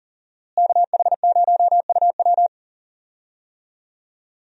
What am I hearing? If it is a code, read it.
Morse code, heard as a keyed tone, about 40 wpm, 700 Hz. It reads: KH0UW
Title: USA Call Signs - 5 Characters - Repeated 3x - 40wpm